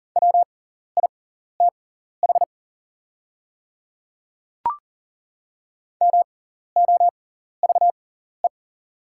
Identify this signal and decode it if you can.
Morse code, heard as a keyed tone, about 40 wpm, spaced as for 14 wpm, 690 Hz. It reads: WITH MOVE
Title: 6x Character Spacing - Top 500 Words - 40wpm